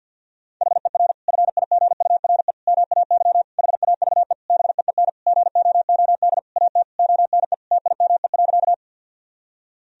Morse code, heard as a keyed tone, about 52 wpm, 700 Hz. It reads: HER FIGURE MAY HAVE BEEN GOOD AT ONE TIME.
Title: Sentences from Top 1000 Words - Review Part 2 - 50wpm